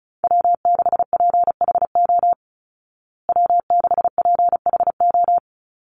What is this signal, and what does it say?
Morse code, heard as a keyed tone, about 34 wpm, 700 Hz. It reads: W6PHO W6PHO
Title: Call Signs Repeated 2x - 35wpm